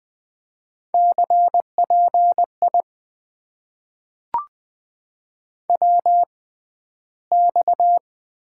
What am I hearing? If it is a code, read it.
Morse code, heard as a keyed tone, about 20 wpm, spaced as for 12 wpm, 700 Hz. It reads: CPI WX
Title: Top 90 QSO Elements - 6x Character Spacing - 20wpm